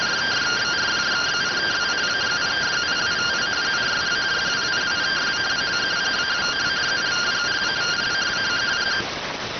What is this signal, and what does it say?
Two-tone teleprinter signal, about 170 Hz shift.